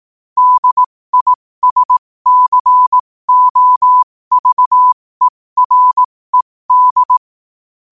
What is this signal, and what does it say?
Morse code, heard as a keyed tone, about 18 wpm, spaced as for 15 wpm, 1000 Hz. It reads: DISCOVERED
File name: qotdfiles / 15wpm